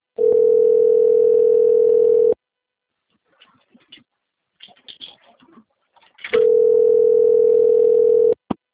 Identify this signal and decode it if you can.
Ringback tone (North America), 440 + 480 Hz.